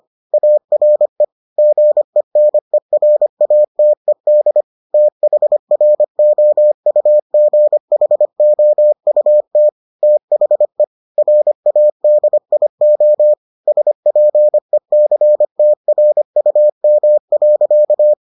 Morse code, heard as a keyed tone, about 25 wpm, 600 Hz. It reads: ARE GENERATED THROUGHOUT THE RADIO SPECTRUM.